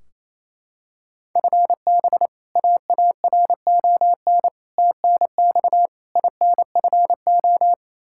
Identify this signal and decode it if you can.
Morse code, heard as a keyed tone, about 28 wpm, 700 Hz. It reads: FB AARON TNX INFO
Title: Practice QSO - Exchange 3 - More Info - 28wpm